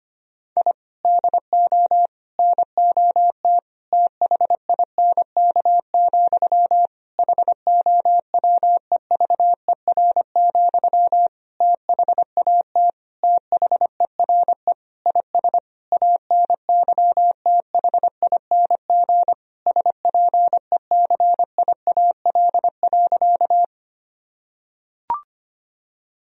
Morse code, heard as a keyed tone, about 25 wpm, 700 Hz. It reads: I DO NOT THINK, HOWEVER, THAT THERE IS ANYTHING SPECIAL.
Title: Sentences from Top 1000 Words - Review Part 3 - 25wpm